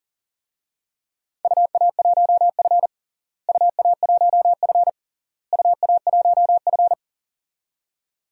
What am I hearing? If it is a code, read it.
Morse code, heard as a keyed tone, about 40 wpm, 700 Hz. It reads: UA1F UA1F UA1F
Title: DXCC-100 Call Signs - 4 Characters - Repeated 3x - 40wpm